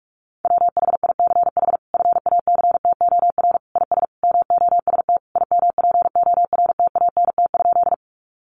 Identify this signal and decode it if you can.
Morse code, heard as a keyed tone, about 44 wpm, 700 Hz. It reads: WHICH FACTOR IS MOST IMPORTANT?
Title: Sentences from Top 1300 Words 45wpm